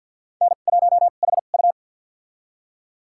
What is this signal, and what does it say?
Morse code, heard as a keyed tone, about 49 wpm, spaced as for 36 wpm, 690 Hz. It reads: N1HU
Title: USA Call Signs - 2x Character Spacing - 50wpm